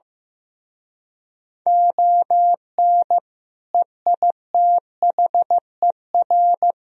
Morse code, heard as a keyed tone, about 16 wpm, 700 Hz. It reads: ON EITHER
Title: Speed-Racing - Sets of 5 Words 15wpm